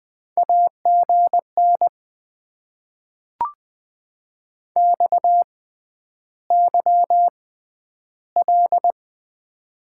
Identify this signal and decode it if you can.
Morse code, heard as a keyed tone, about 20 wpm, spaced as for 10 wpm, 700 Hz. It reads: AGN XYL
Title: Top 90 QSO Elements - 6x Character Spacing - 20wpm